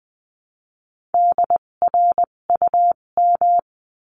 Morse code, heard as a keyed tone, about 20 wpm, spaced as for 17 wpm, 700 Hz. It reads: DRUM
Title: 257 Lesson 42b - Words 14-20wpm